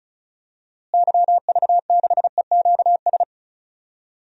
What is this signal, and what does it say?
Morse code, heard as a keyed tone, about 34 wpm, 700 Hz. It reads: YV6EQS
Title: DXCC-100 Call Signs - 6 Characters - Repeated 3x - 35wpm